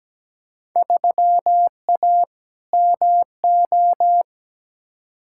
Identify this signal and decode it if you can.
Morse code, heard as a keyed tone, about 17 wpm, 700 Hz. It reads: 3A MO
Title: Field Day Exchanges Encore - 17wpm